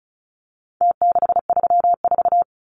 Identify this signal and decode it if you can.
Morse code, heard as a keyed tone, about 34 wpm, 700 Hz. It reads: T634
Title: Numbers - Contest Style - 4-Digit 35wpm